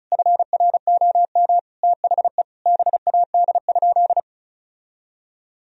Morse code, heard as a keyed tone, about 36 wpm, 700 Hz. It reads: FROM THE BAD?
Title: Sentences from Top 1000 Words - Review Part 1 - 35wpm